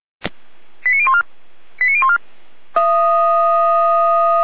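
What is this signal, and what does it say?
Five-tone selective call (ZVEI), heard as quick runs of five tones, tones of about 70 ms. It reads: 89014 89014
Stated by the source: ZVEI five-tone selective call received with an SDR, from the Feuerwehr Orlamünde audio server.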